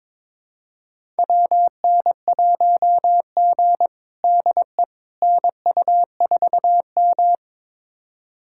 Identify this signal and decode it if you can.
Morse code, heard as a keyed tone, about 22 wpm, 700 Hz. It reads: WN1G DE NU4M
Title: USA Callsign DE Callsign - 4 Characters - 22wpm